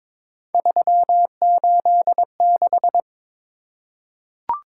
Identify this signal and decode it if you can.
Morse code, heard as a keyed tone, about 22 wpm, 700 Hz. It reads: 386
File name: practice / 22wpm / Three Letter-Number Combinations - 22wpm